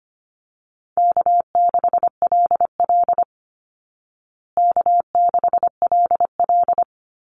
Morse code, heard as a keyed tone, about 25 wpm, 700 Hz. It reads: K6LL K6LL
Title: Call Signs Repeated 3x - 25wpm